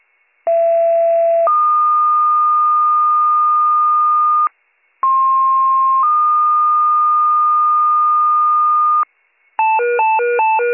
Two-tone page, heard as a one-second tone, then a three-second tone, about 670 then 1220 Hz, about 1060 then 1220 Hz. It is heard from 0.5 to 9.0 s.